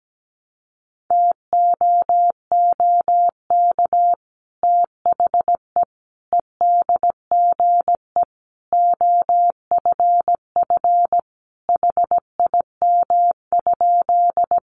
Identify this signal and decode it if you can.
Morse code, heard as a keyed tone, about 17 wpm, 700 Hz. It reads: TOOK THE EDGE OFF HIM?
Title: Sentences from Top 1300 Words 17wpm